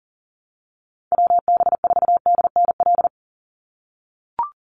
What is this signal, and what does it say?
Morse code, heard as a keyed tone, about 40 wpm, 700 Hz. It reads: WB4DNL
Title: Call Signs - 40wpm